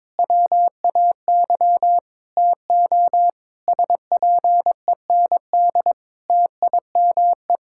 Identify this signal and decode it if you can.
Morse code, heard as a keyed tone, about 22 wpm, 700 Hz. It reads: WAY TO SPEND TIME
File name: Difficult Sentences with Hints - 22wpm